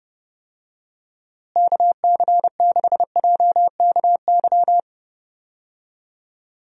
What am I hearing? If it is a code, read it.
Morse code, heard as a keyed tone, about 30 wpm, 700 Hz. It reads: KC6JKY
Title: USA Call Signs - 6 Characters - Repeated 2x - 30wpm